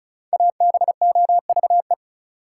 Morse code, heard as a keyed tone, about 36 wpm, 700 Hz. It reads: ABOVE
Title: Sentences from Top 1200 Words - New Word Drill Alt - 35wpm